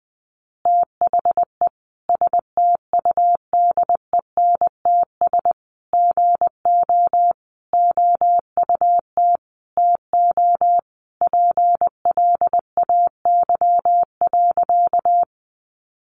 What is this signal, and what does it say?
Morse code, heard as a keyed tone, about 20 wpm, 700 Hz. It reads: THE STUDENTS GO OUT TO PLAY.
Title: Easy Phrases 20wpm